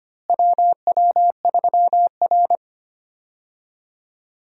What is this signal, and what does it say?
Morse code, heard as a keyed tone, about 25 wpm, 700 Hz. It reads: WW3R
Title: USA Call Signs - 4 Characters - Repeated 2x - 25wpm